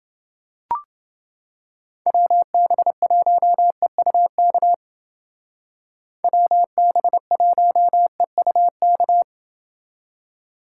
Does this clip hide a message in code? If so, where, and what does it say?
Yes: Morse code from 2.1 to 9.2 s, heard as a keyed tone, about 28 wpm, 700 Hz. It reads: WB1EUK WB1EUK